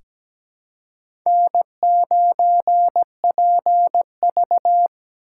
Morse code, heard as a keyed tone, about 17 wpm, 700 Hz. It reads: N9PV
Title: USA Call Signs - 4x Character Spacing - 17wpm